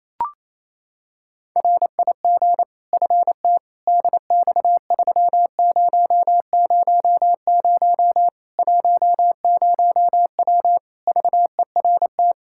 Morse code, heard as a keyed tone, about 27 wpm, 700 Hz. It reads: RIG FT DX3000 10W VERT
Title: Practice QSO - Exchange 2 - Info - 28wpm